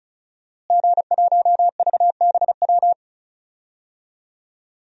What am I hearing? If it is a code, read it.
Morse code, heard as a keyed tone, about 34 wpm, 700 Hz. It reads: G1VBW
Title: DXCC-100 Call Signs - 5 Characters - Repeated 2x - 35wpm